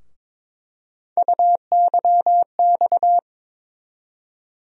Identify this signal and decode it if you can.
Morse code, heard as a keyed tone, about 21 wpm, 700 Hz. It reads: UYX
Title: USA Call Sign Postfixes - 3 Letters 22wpm